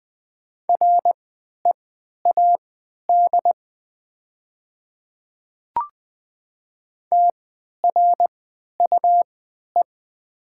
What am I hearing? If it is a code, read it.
Morse code, heard as a keyed tone, about 20 wpm, spaced as for 11 wpm, 700 Hz. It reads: READ TRUE
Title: 3x Character Spacing - Top 500 Words - 20wpm